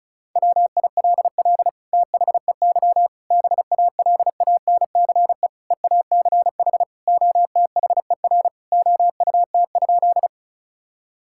Morse code, heard as a keyed tone, about 36 wpm, 700 Hz. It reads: WILL THEY BALANCE EACH OTHER OUT?
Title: Sentences from Top 1000 Words - Review Part 6 - 35wpm